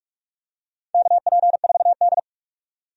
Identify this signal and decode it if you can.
Morse code, heard as a keyed tone, about 44 wpm, 700 Hz. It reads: KP4D
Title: DXCC-100 Call Signs - 4 Characters - 45wpm